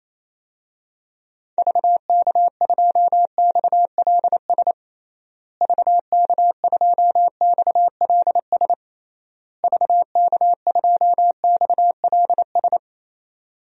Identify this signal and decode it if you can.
Morse code, heard as a keyed tone, about 28 wpm, 700 Hz. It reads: VK2XLS VK2XLS VK2XLS
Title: DXCC-100 Call Signs - 6 Characters - Repeated 3x - 28wpm